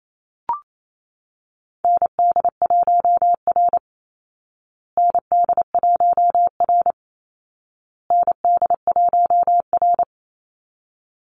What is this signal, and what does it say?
Morse code, heard as a keyed tone, about 28 wpm, 700 Hz. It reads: ND1R ND1R ND1R
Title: Call Signs Repeated 2x - 28wpm